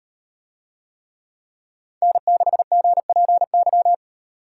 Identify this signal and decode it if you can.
Morse code, heard as a keyed tone, about 38 wpm, 700 Hz. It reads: N6GPY